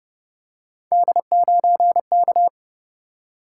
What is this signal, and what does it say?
Morse code, heard as a keyed tone, about 30 wpm, 700 Hz. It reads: D9K